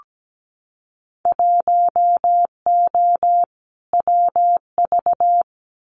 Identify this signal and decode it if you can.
Morse code, heard as a keyed tone, about 17 wpm, 700 Hz. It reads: 1O WV
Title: Winter Field Day Exchanges 17wpm